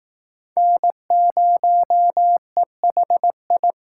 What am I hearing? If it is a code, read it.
Morse code, heard as a keyed tone, about 18 wpm, 700 Hz. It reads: N0EHI
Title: Speed Racing - USA 1x3 Call Signs 15wpm